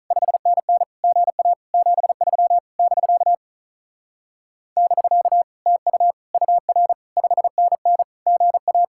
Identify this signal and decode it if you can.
Morse code, heard as a keyed tone, about 46 wpm, 700 Hz. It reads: 5NN GA 73 <BK> <BK> TU UR 5NN GA